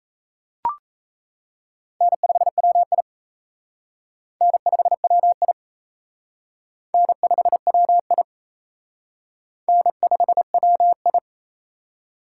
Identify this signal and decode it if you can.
Morse code, heard as a keyed tone, about 34 wpm, 700 Hz. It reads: N5WI N5WI N5WI N5WI
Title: Speed Racing - USA 1x2 Call Signs 28wpm